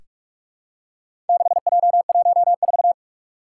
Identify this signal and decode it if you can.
Morse code, heard as a keyed tone, about 44 wpm, 700 Hz. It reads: 6J14
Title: Four Challenging Letter-Number Combinations 45wpm